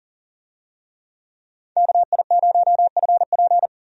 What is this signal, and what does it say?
Morse code, heard as a keyed tone, about 40 wpm, 700 Hz. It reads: KI0FP
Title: USA Call Signs - 5 Characters - 40wpm